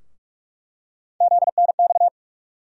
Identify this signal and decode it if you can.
Morse code, heard as a keyed tone, about 46 wpm, 700 Hz. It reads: ZNX
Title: USA Call Sign Postfixes - 3 Letters 45wpm